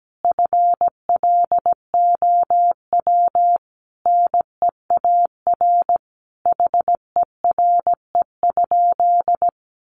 Morse code, heard as a keyed tone, about 17 wpm, 700 Hz. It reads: FLOW NEAR HERE?
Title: Sentences from Top 1200 Words 17wpm